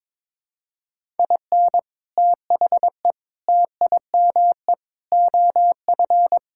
Morse code, heard as a keyed tone, about 22 wpm, 700 Hz. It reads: IN THE TIME OF
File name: practice / 22wpm / Sentences with 3-Word Phrases - 22wpm